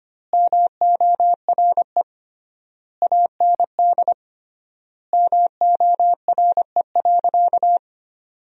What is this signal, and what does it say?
Morse code, heard as a keyed tone, about 25 wpm, 700 Hz. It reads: MORE AND MORE.
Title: 3x Word-Spacing - Sentences from Top 500 Words - 25wpm